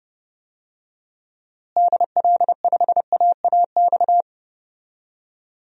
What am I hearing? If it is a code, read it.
Morse code, heard as a keyed tone, about 31 wpm, 700 Hz. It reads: DL5AAX
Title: DXCC-100 Call Signs - 6 Characters - Repeated 2x - 30wpm